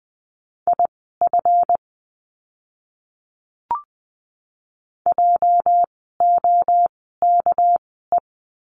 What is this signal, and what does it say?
Morse code, heard as a keyed tone, about 20 wpm, spaced as for 14 wpm, 700 Hz. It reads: IF JOKE
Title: 206 Lesson 33f - Letter j - Words 20wpm